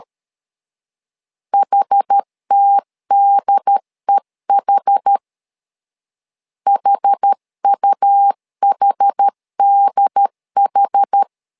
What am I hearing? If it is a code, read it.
Morse code, heard as a keyed tone, about 13 wpm, 800 Hz. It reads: HTDEH HUHDH
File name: LEZIONI / 2 UHDN / 2 UHD 12wpm 1540